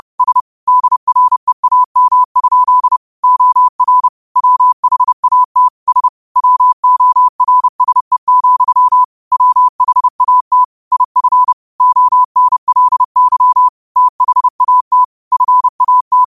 Morse code, heard as a keyed tone, about 30 wpm, 1000 Hz. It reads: S DREAM? OR WHAT S WORSE, WHAT IF ONLY THAT FAT
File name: 30wpm